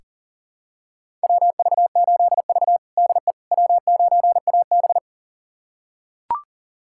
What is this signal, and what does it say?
Morse code, heard as a keyed tone, about 40 wpm, 700 Hz. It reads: WV8V DE W9AB
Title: USA Callsign DE Callsign - 4 Characters - 40wpm